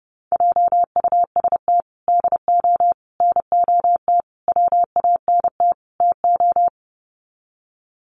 Morse code, heard as a keyed tone, about 30 wpm, 700 Hz. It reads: JUST DO NOT WANT TO
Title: Sets of 5 Words Encore 30wpm